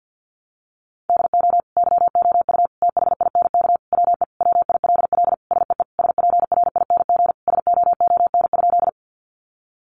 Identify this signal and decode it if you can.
Morse code, heard as a keyed tone, about 50 wpm, 700 Hz. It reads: DO YOU THINK WE WILL SEE SPRING SOON?
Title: Sentences from Top 1200 Words 50wpm